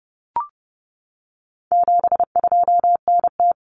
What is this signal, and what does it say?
Morse code, heard as a keyed tone, about 30 wpm, 700 Hz. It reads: 72NT
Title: Common Cut Numbers - 4-Digit 30wpm